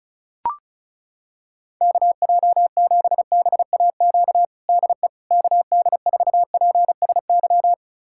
Morse code, heard as a keyed tone, about 36 wpm, 700 Hz. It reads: KJ7BAQ DE KD4PSY
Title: USA Callsign DE Callsign 35wpm